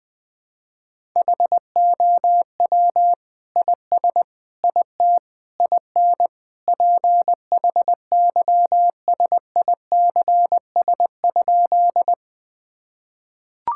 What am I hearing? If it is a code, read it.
Morse code, heard as a keyed tone, about 20 wpm, 700 Hz. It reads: HOW IS IT IN PHYSICS?
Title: Sentences from Top 1000 Words - Review Part 5 - 20wpm